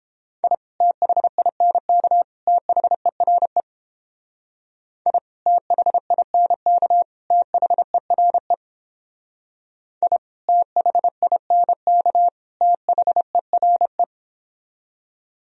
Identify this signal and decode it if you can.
Morse code, heard as a keyed tone, about 29 wpm, 700 Hz. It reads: I THINK THERE I THINK THERE I THINK THERE